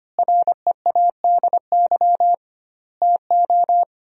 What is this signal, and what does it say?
Morse code, heard as a keyed tone, about 25 wpm, 700 Hz. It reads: READY TO
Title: 2x Word-Spacing - Sentences from Top 400 Words - 25wpm